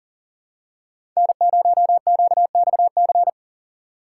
Morse code, heard as a keyed tone, about 40 wpm, 700 Hz. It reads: N0QXC
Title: USA Call Signs - 5 Characters - 6x Character Spacing - 40wpm